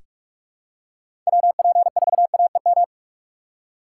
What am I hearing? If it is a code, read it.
Morse code, heard as a keyed tone, about 43 wpm, 700 Hz. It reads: WP4AEM
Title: USA Call Signs - 6 Characters - 3x Character Spacing - 45wpm